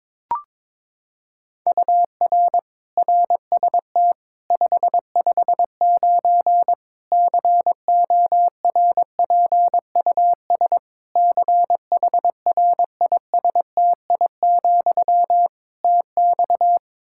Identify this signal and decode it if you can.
Morse code, heard as a keyed tone, about 22 wpm, 700 Hz. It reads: UR RST 559 CORPUS CHRISTI, TX
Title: Practice QSO - Exchange 1 - Beginning - 22wpm